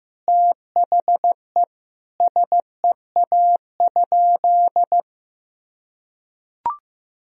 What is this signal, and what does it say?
Morse code, heard as a keyed tone, about 15 wpm, 700 Hz. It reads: THE SEA?
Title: Sentences from Top 1000 Words - Review Part 1 - 15wpm